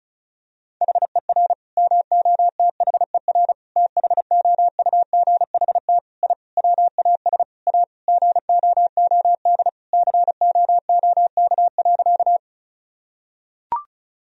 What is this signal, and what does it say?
Morse code, heard as a keyed tone, about 36 wpm, 700 Hz. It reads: HER MOTHER THOUGHT I WAS A GOOD COOK.
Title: Sentences from Top 1000 Words - Review Part 4 - 35wpm